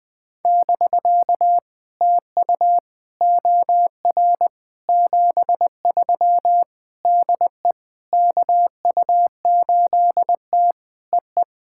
Morse code, heard as a keyed tone, about 20 wpm, 700 Hz. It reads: <BK> TU OR 73 DE KU8T EE
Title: POTA QSO Practice - 20wpm